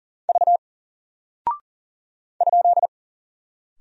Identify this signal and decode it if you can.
Morse code, heard as a keyed tone, about 40 wpm, 700 Hz. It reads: V ?
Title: Single Character (Letter:Number:Punctuation:Prosign) Alt 40wpm